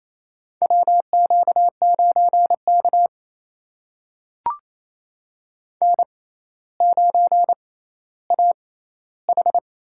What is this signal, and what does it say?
Morse code, heard as a keyed tone, about 28 wpm, spaced as for 14 wpm, 700 Hz. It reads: WQ9K N9AH
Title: USA Call Signs - 4 Characters - 6x Character Spacing - 28wpm